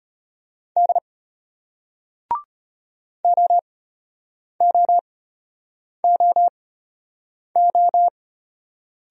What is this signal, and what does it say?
Morse code, heard as a keyed tone, about 34 wpm, spaced as for 8 wpm, 700 Hz. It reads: D OOOO